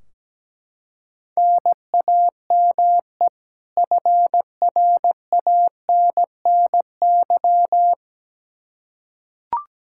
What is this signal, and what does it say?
Morse code, heard as a keyed tone, about 17 wpm, 700 Hz. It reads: NAME FRANNY
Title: Practice QSO - Exchange 1 - Beginning - 17wpm